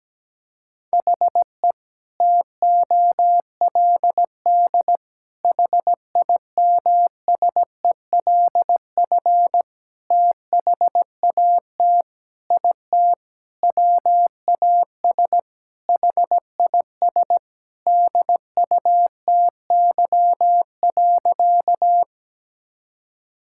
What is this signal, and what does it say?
Morse code, heard as a keyed tone, about 17 wpm, 700 Hz. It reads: HE TOLD HIMSELF THAT IT WAS HIS DUTY.
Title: Sentences from Top 1600 Words 17wpm